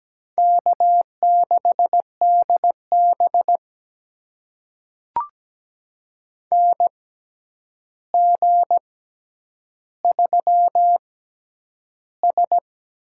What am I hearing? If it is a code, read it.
Morse code, heard as a keyed tone, about 17 wpm, spaced as for 9 wpm, 700 Hz. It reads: K6DB NG3S
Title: USA Call Signs - 4 Characters - 6x Character Spacing - 17wpm